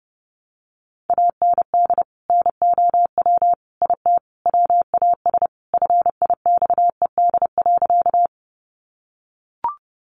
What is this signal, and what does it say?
Morse code, heard as a keyed tone, about 28 wpm, 700 Hz. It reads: AND NOW IT WAS FIXED.
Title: Sentences from Top 1100 Words 30wpm